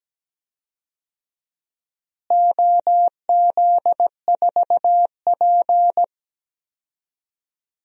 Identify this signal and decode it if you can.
Morse code, heard as a keyed tone, about 17 wpm, 700 Hz. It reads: OZ4P
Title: DXCC-100 Call Signs - 4 Characters - Repeated 2x - 17wpm